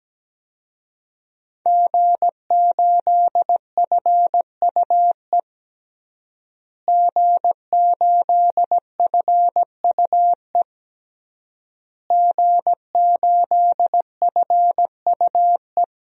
Morse code, heard as a keyed tone, about 17 wpm, 700 Hz. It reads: G8FUE G8FUE G8FUE
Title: DXCC-100 Call Signs - 5 Characters - Repeated 3x - 17wpm